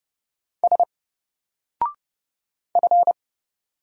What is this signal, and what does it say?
Morse code, heard as a keyed tone, about 30 wpm, 700 Hz. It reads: S F
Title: Single Character (Letter:Number:Punctuation:Prosign) 30wpm